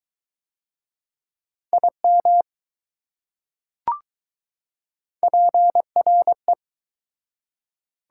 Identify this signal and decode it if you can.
Morse code, heard as a keyed tone, about 21 wpm, 700 Hz. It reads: IM PRE